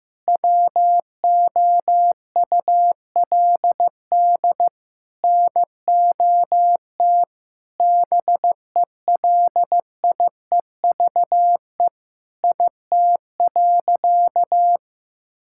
Morse code, heard as a keyed tone, about 15 wpm, 700 Hz. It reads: WOULD NOT BELIEVE IT.
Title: Sentences from Top 1000 Words - Review Part 6 - 15wpm